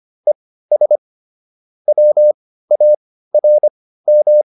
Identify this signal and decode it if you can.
Morse code, heard as a keyed tone, about 25 wpm, spaced as for 15 wpm, 600 Hz. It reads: ES WARM
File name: QSO307 15